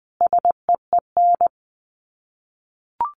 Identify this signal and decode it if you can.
Morse code, heard as a keyed tone, about 20 wpm, 700 Hz. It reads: SEEN
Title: Sets of 5 Words Encore 20wpm